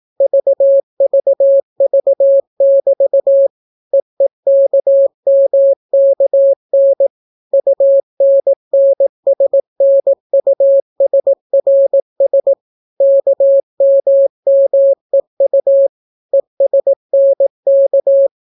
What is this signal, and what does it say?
Morse code, heard as a keyed tone, about 18 wpm, 550 Hz. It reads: VVV<BT> EEKMKN UNNSNUSRS KMMEU ESNK